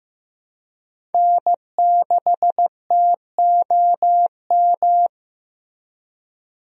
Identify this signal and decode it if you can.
Morse code, heard as a keyed tone, about 15 wpm, 700 Hz. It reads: N6TOM
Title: USA Call Signs - 5 Characters - Repeated 2x - 15wpm